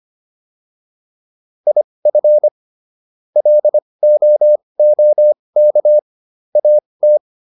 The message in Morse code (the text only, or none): IF LOOK AT